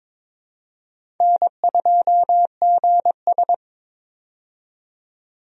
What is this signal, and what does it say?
Morse code, heard as a keyed tone, about 22 wpm, 700 Hz. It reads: N2GS
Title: USA Call Signs - 4 Characters - 22wpm